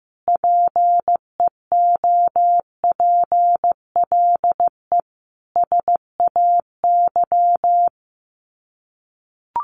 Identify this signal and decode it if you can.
Morse code, heard as a keyed tone, about 15 wpm, 700 Hz. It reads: PEOPLE SAY
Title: Sets of 5 Words Encore2 15wpm